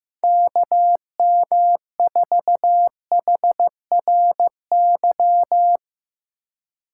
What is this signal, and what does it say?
Morse code, heard as a keyed tone, about 15 wpm, 700 Hz. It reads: KM4HRY
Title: USA Call Signs - 6 Characters - 2x Character Spacing - 15wpm